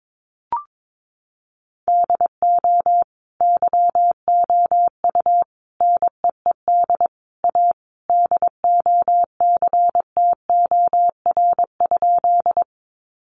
Morse code, heard as a keyed tone, about 22 wpm, 700 Hz. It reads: DO YOU NEED A DOCTOR?
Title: Sentences from Top 1500 Words 22wpm